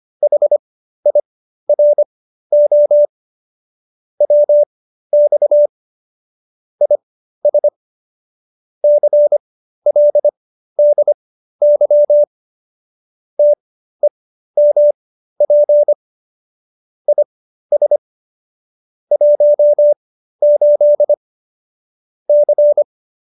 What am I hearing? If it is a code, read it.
Morse code, heard as a keyed tone, about 25 wpm, spaced as for 13 wpm, 590 Hz. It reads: HIRO WX IS CLDY TEMP IS 18 C
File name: qso208b 13